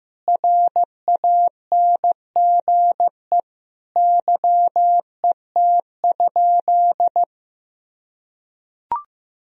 Morse code, heard as a keyed tone, about 16 wpm, 700 Hz. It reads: RANGE YET?
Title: Sentences from Top 1000 Words - Review Part 4 - 15wpm